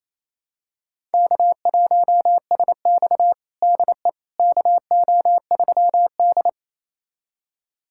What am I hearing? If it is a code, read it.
Morse code, heard as a keyed tone, about 28 wpm, 700 Hz. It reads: K1SX DE KO3D
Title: USA Callsign DE Callsign - 4 Characters - 28wpm